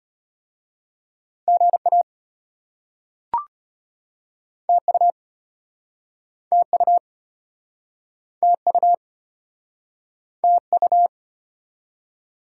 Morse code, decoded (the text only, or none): GA TU TU TU TU